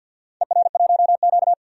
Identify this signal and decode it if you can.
Morse code, heard as a keyed tone, about 52 wpm, 700 Hz. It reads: ER1Q
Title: DXCC-100 Call Signs - 4 Characters - 50wpm